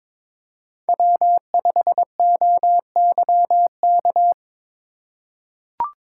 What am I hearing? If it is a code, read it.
Morse code, heard as a keyed tone, about 22 wpm, 700 Hz. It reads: W5OYK
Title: USA Call Signs - 4x Character Spacing - 22wpm